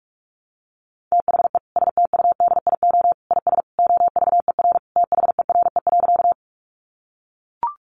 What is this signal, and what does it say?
Morse code, heard as a keyed tone, about 44 wpm, 700 Hz. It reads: THE STUDIO IS OVER THERE.
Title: Sentences from Top 1200 Words 45wpm